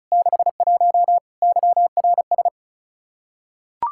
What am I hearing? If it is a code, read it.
Morse code, heard as a keyed tone, about 34 wpm, 700 Hz. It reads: 61 YRS
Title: Practice QSO - Exchange 3 - More Info - 35wpm